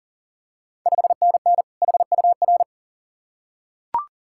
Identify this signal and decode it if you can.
Morse code, heard as a keyed tone, about 39 wpm, 690 Hz. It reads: 5NN HUR
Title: Ontario QSO Party Exchange 40wpm